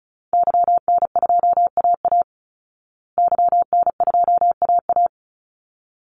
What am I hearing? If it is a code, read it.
Morse code, heard as a keyed tone, about 34 wpm, 700 Hz. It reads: YN2AA YN2AA
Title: Call Signs Repeated 3x - 35wpm